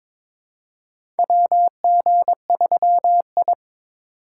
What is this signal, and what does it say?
Morse code, heard as a keyed tone, about 22 wpm, 700 Hz. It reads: WG3I
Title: USA Call Signs - 4 Characters - 3x Character Spacing - 22wpm